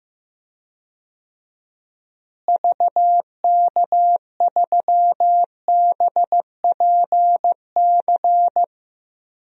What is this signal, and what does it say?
Morse code, heard as a keyed tone, about 15 wpm, 700 Hz. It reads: VK3BPC
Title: DXCC-100 Call Signs - 6 Characters - Repeated 3x - 15wpm